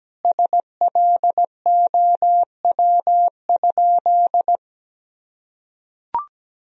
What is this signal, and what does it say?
Morse code, heard as a keyed tone, about 17 wpm, 700 Hz. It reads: SLOW?
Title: Sentences from Top 1000 Words - Review Part 2 - 17wpm